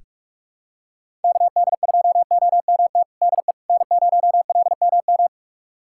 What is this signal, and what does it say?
Morse code, heard as a keyed tone, about 47 wpm, 700 Hz. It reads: KD2OMT DE N0LMM
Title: USA Callsign DE Callsign 45wpm